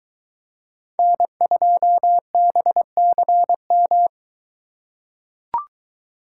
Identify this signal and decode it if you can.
Morse code, heard as a keyed tone, about 22 wpm, 700 Hz. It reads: N2BCM